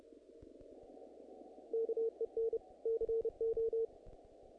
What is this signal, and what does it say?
Morse code, heard as a keyed tone, about 32 wpm, 470 Hz. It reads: KEN CO
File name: contests / naqp-cw-k5tr-op-k5pi- -1900